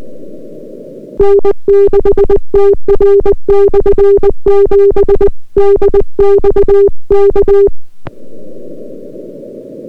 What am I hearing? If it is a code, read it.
Morse code, heard as a keyed tone, about 20 wpm, 400 Hz. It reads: N6TR/7 DX K